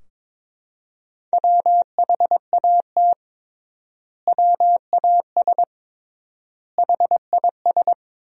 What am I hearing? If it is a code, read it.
Morse code, heard as a keyed tone, about 22 wpm, 700 Hz. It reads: WHAT WAS HIS
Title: 3x Word-Spacing - Sentences from Top 200 Words - 22wpm